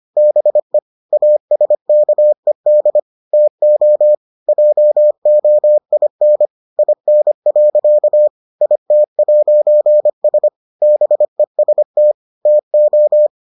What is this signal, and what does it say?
Morse code, heard as a keyed tone, about 25 wpm, 600 Hz. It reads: BE ASKED TO JOIN IN. IT'S BEST TO